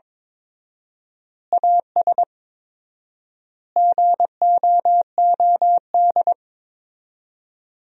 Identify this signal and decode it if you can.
Morse code, heard as a keyed tone, about 22 wpm, 700 Hz. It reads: AS GOOD
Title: 4x Word-Spacing - Sentences from Top 200 Words - 22wpm